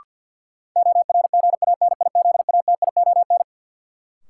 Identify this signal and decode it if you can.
Morse code, heard as a keyed tone, about 53 wpm, 700 Hz. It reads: ORGANIZATION
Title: Sentences from Top 1000 Words - Review Part 5 - New Word Drill - 50wpm